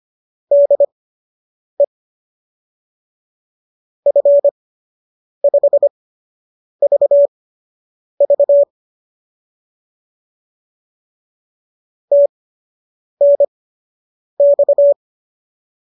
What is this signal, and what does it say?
Morse code, heard as a keyed tone, about 25 wpm, spaced as for 8 wpm, 570 Hz. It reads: DE F5VV TNX